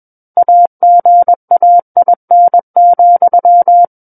Morse code, heard as a keyed tone, about 21 wpm, 700 Hz. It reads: AGAIN,
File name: Book One / 12-Chapter 12 - 21 WPM